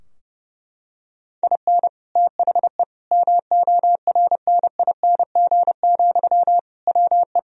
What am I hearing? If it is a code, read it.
Morse code, heard as a keyed tone, about 30 wpm, 700 Hz. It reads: IN THE MORNING, WE
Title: Sentences from Top 1000 Words - Review Part 3 - 30wpm